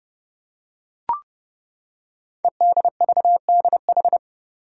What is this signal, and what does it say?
Morse code, heard as a keyed tone, about 30 wpm, 700 Hz. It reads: EDVDH
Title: Code Groups - 5 Letters - 30wpm